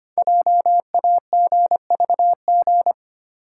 Morse code, heard as a keyed tone, about 25 wpm, 700 Hz. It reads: JAGVG
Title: Code Groups - 5 Letters - 25wpm